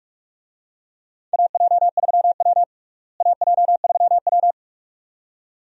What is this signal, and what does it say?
Morse code, heard as a keyed tone, about 45 wpm, 700 Hz. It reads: AJ3W AJ3W
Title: USA Call Signs - 4 Characters - Repeated 2x - 45wpm